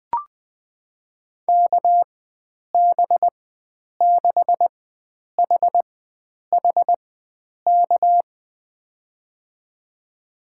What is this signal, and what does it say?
Morse code, heard as a keyed tone, about 20 wpm, spaced as for 9 wpm, 700 Hz. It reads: KB6HHK
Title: USA Call Signs - 6 Characters - 4x Character Spacing - 20wpm